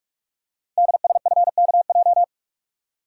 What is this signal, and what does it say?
Morse code, heard as a keyed tone, about 44 wpm, 700 Hz. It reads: DSFKJ
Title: Code Groups - 5 Letters - 45wpm